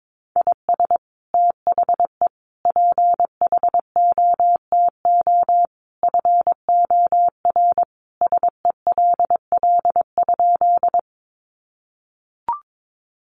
Morse code, heard as a keyed tone, about 22 wpm, 700 Hz. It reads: IS THE PHOTO FOR SELL?
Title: Sentences from Top 1200 Words 22wpm